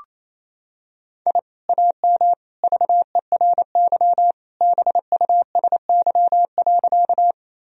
Morse code, heard as a keyed tone, about 28 wpm, 700 Hz. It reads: I AM VERY BUSY.
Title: Sentences from Top 1000 Words - Review Part 3 - 28wpm